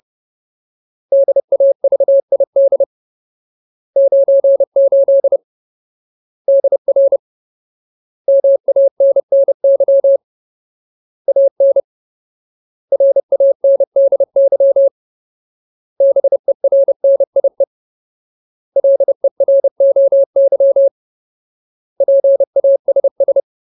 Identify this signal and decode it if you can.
Morse code, heard as a keyed tone, about 30 wpm, 550 Hz. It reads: DAVID 98 DR MANNY AN RANDY BERNIE LEROY PASS
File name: Ultimate Word List / 30WPM Ultimate